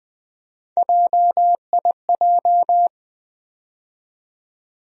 Morse code, heard as a keyed tone, about 20 wpm, 700 Hz. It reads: JIJ
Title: USA Call Sign Postfixes - 3 Letters 20wpm